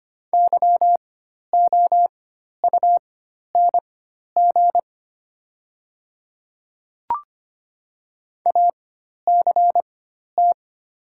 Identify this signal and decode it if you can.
Morse code, heard as a keyed tone, about 25 wpm, spaced as for 12 wpm, 700 Hz. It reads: YOUNG ACT
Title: 4x Character Spacing - Top 500 Words - 25wpm